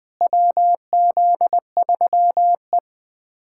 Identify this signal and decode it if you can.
Morse code, heard as a keyed tone, about 20 wpm, 700 Hz. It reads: WZ3E
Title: USA Callsign DE Callsign - 4 Characters - 20wpm